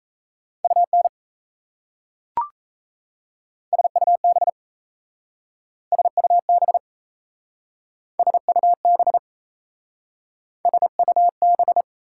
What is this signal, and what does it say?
Morse code, heard as a keyed tone, about 36 wpm, 700 Hz. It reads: UN SUB SUB SUB SUB